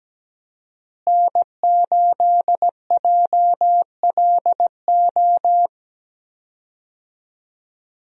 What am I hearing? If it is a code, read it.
Morse code, heard as a keyed tone, about 17 wpm, 700 Hz. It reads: N8JLO
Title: USA Call Signs - 5 Characters - Repeated 3x - 17wpm